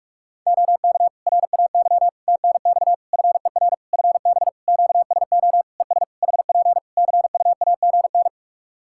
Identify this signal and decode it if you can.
Morse code, heard as a keyed tone, about 46 wpm, 700 Hz. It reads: OK RAY TNX FER FB QSO ES HP CUAGN
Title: Practice QSO - Exchange 4 - Ending - 45wpm